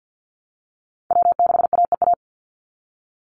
Morse code, heard as a keyed tone, about 50 wpm, 700 Hz. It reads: W6AEA
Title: Call Signs Repeated 2x - 50wpm